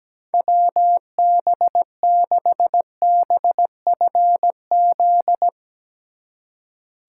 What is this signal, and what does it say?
Morse code, heard as a keyed tone, about 17 wpm, 700 Hz. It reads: WB6BFZ
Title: Speed Racing - USA 2x3 Call Signs 17wpm